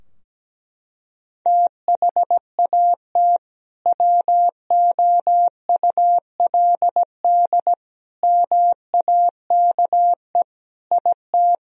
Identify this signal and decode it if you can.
Morse code, heard as a keyed tone, about 17 wpm, 700 Hz. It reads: THAT WOULD MAKE IT
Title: Sentences from Top 1000 Words - Review Part 2 - 17wpm